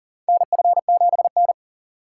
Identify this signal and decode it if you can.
Morse code, heard as a keyed tone, about 40 wpm, 700 Hz. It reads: NF7N
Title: USA Call Signs - 4 Characters - 6x Character Spacing - 40wpm